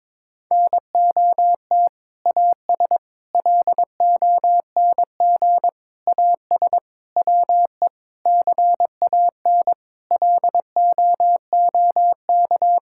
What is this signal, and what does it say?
Morse code, heard as a keyed tone, about 22 wpm, 700 Hz. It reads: NOT AS LONG AS WE CAN LOOK
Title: Sentences from Top 1000 Words - Review Part 6 - 22wpm